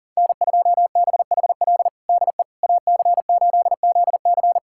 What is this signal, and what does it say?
Morse code, heard as a keyed tone, about 40 wpm, 700 Hz. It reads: N2BHL DE AC8ZC
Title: USA Callsign DE Callsign - 5 Characters - 40wpm